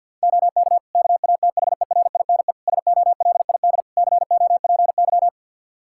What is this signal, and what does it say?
Morse code, heard as a keyed tone, about 47 wpm, 700 Hz. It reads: OK KATHERINE SOLID COPY
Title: Practice QSO - Exchange 3 - More Info - 50wpm